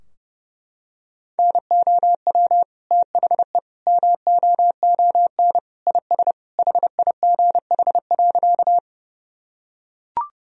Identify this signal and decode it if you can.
Morse code, heard as a keyed tone, about 30 wpm, 700 Hz. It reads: NOW THE MOON IS HIGH.
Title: Sentences from Top 1000 Words - Review Part 3 - 30wpm